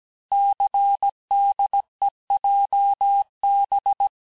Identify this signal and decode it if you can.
Morse code, heard as a keyed tone, about 17 wpm, 800 Hz. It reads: CDEJB
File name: corrige-cours-1581-texte-chiffre-vitesse-16-mpm